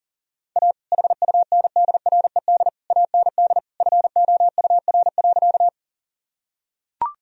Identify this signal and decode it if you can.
Morse code, heard as a keyed tone, about 38 wpm, 700 Hz. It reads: A HUNDRED AND FOUR.
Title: Sentences from Top 1000 Words - Review Part 2 - 40wpm